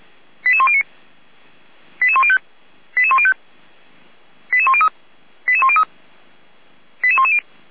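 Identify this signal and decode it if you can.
Five-tone selective call (ZVEI), heard as quick runs of five tones, tones of about 70 ms. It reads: 88109 88105 88105 88103 88103 88110
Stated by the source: ZVEI five-tone call received by SDR, recorded by feuerwehr-orlamuende.org (fire brigade Orlamünde).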